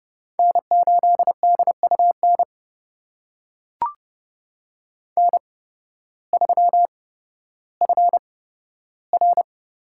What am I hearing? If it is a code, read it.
Morse code, heard as a keyed tone, about 31 wpm, spaced as for 14 wpm, 700 Hz. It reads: N8DUN N3FR
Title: USA Call Signs - 5 Characters - 8x Character Spacing - 30wpm